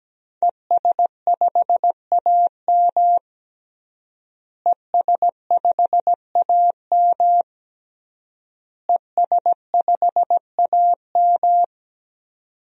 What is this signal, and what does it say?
Morse code, heard as a keyed tone, about 17 wpm, 700 Hz. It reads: ES5AM ES5AM ES5AM
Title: DXCC-100 Call Signs - 5 Characters - Repeated 3x - 17wpm